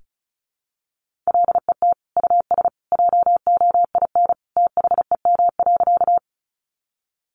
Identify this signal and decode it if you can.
Morse code, heard as a keyed tone, about 34 wpm, 700 Hz. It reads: LET US JOIN THEM.
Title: Sentences from Top 1400 Words 35wpm